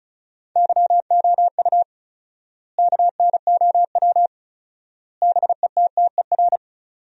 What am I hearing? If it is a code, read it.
Morse code, heard as a keyed tone, about 36 wpm, 700 Hz. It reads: YOU KNOW BETTER
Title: 4x Word-Spacing - Sentences from Top 400 Words - 35wpm